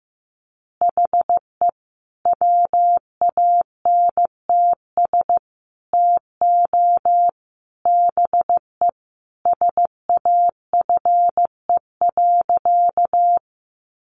Morse code, heard as a keyed tone, about 14 wpm, 700 Hz. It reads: HE WANTS TO BE SAFE.
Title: Additional Easy Phrases 15wpm